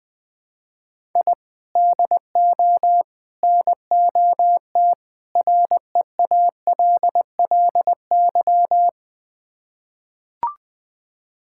Morse code, heard as a keyed tone, about 21 wpm, 700 Hz. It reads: I DO NOT REALLY
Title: Build A Phrase - 4-Word Phrases - 20wpm